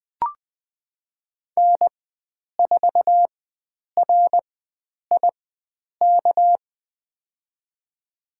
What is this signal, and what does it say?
Morse code, heard as a keyed tone, about 20 wpm, spaced as for 9 wpm, 700 Hz. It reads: N4RIK
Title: USA Call Signs - 5 Characters - 4x Character Spacing - 20wpm